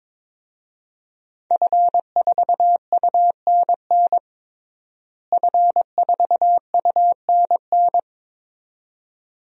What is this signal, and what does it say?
Morse code, heard as a keyed tone, about 22 wpm, 700 Hz. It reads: F4UNN F4UNN
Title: DXCC-100 Call Signs - 5 Characters - Repeated 2x - 22wpm